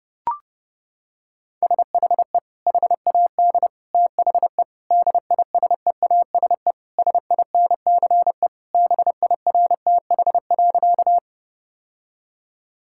Morse code, heard as a keyed tone, about 28 wpm, 700 Hz. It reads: SHE HAD THE DISEASE SINCE BIRTH.
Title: Sentences from Top 1000 Words - Review Part 6 - 30wpm